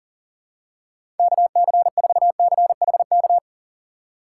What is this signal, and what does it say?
Morse code, heard as a keyed tone, about 40 wpm, 700 Hz. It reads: KC4CHK
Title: USA Call Signs - 3x Character Spacing - 40wpm